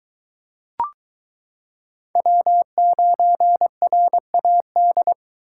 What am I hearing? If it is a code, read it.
Morse code, heard as a keyed tone, about 22 wpm, 700 Hz. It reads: W9RAD